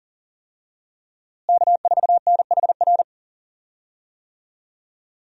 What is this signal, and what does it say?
Morse code, heard as a keyed tone, about 40 wpm, 700 Hz. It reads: K4NHR
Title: USA Call Signs - 5 Characters - Repeated 3x - 40wpm